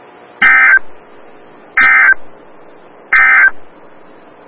Emergency Alert System (SAME) tones, heard as harsh buzzing bursts. It is heard from 0.3 to 3.6 s.
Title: Des Moines, IA NOAA Weather Radio - WXL57